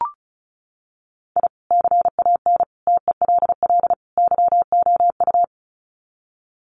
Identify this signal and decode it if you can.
Morse code, heard as a keyed tone, about 34 wpm, 700 Hz. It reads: I CAN TELL YOU
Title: Sets of 4 Words 35wpm